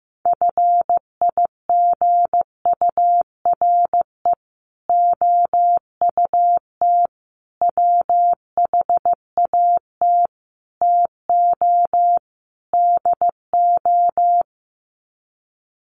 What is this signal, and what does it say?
Morse code, heard as a keyed tone, about 15 wpm, 700 Hz. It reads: FIGURE OUT WHAT TO DO
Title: Sets of 5 Words Encore2 15wpm